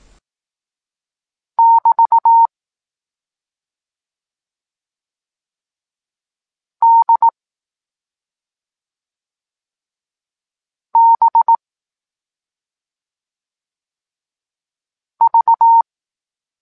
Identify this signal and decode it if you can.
Morse code, heard as a keyed tone, about 18 wpm, 920 Hz. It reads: <BT> D B V
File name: Tekst-26 1